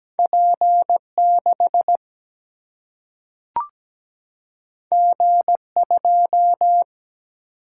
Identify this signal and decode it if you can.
Morse code, heard as a keyed tone, about 17 wpm, 700 Hz. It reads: P6 G2
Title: Single-Letter Single-Number 17wpm